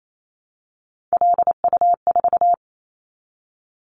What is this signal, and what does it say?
Morse code, heard as a keyed tone, about 28 wpm, 700 Hz. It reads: LU4
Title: Callsign Prefixes for Top 50 DXCC - 28wpm